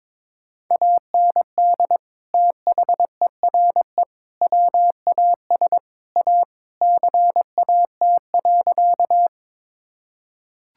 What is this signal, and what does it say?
Morse code, heard as a keyed tone, about 22 wpm, 700 Hz. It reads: AND THERE WAS A CAT.
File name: practice / 22wpm / Sentences with 4-Word Phrases - 22wpm